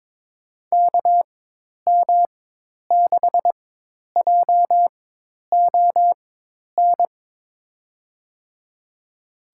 Morse code, heard as a keyed tone, about 22 wpm, spaced as for 10 wpm, 700 Hz. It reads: KM6JON
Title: USA Call Signs - 6 Characters - 4x Character Spacing - 22wpm